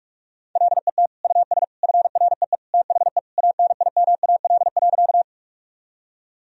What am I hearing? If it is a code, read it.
Morse code, heard as a keyed tone, about 46 wpm, 700 Hz. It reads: LET US FREE THE ANIMAL.
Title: Sentences from Top 1000 Words - Review Part 1 - 45wpm